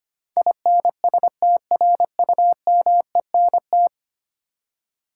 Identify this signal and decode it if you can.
Morse code, heard as a keyed tone, about 25 wpm, 700 Hz. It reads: INSTRUMENT
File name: practice / 25wpm / Ten Letter Words - 25wpm